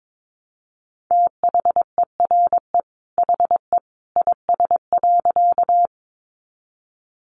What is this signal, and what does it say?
Morse code, heard as a keyed tone, about 22 wpm, 700 Hz. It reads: THERE HE IS.
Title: A Sound Of Thunder 22wpm